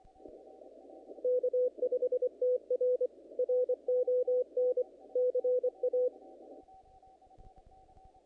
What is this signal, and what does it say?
Morse code, heard as a keyed tone, about 24 wpm, 500 Hz. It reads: K5TR RON CA
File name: contests / naqp-cw-k5tr-op-k5pi- -2000